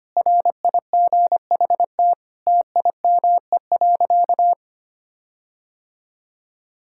Morse code, heard as a keyed tone, about 25 wpm, 700 Hz. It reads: RIGHT TIME.
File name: practice / 25wpm / Sentences with 9-Word Phrases - 25wpm